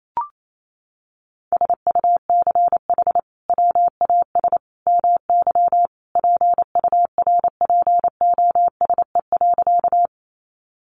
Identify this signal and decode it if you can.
Morse code, heard as a keyed tone, about 28 wpm, 700 Hz. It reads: SUCH WAS MY PURPOSE.
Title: Sentences from Top 800 Words 28wpm